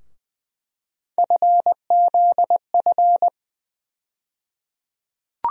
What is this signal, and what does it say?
Morse code, heard as a keyed tone, about 20 wpm, 700 Hz. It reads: FZF
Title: USA Call Sign Postfixes - 3 Letters 20wpm